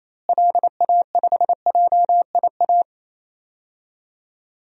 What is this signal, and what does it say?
Morse code, heard as a keyed tone, about 28 wpm, 700 Hz. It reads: LA5JIA
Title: DXCC-100 Call Signs - 6 Characters - Repeated 2x - 28wpm